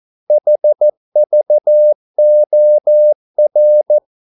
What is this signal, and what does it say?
Morse code, heard as a keyed tone, about 14 wpm, 600 Hz. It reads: HVOR
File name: Korte ord DK 14wpm